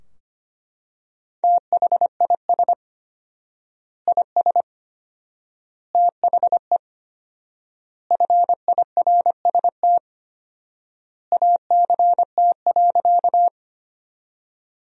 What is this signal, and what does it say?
Morse code, heard as a keyed tone, about 25 wpm, 700 Hz. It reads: THIS IS THE FIRST ACT.
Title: 4x Word-Spacing - Sentences from Top 200 Words - 25wpm